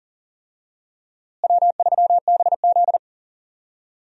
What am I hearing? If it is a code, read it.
Morse code, heard as a keyed tone, about 38 wpm, 700 Hz. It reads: W3BZ
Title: USA Call Signs - 4 Characters - Repeated 3x - 40wpm